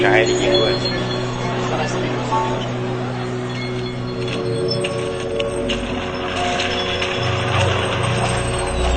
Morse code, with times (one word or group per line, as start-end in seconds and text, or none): none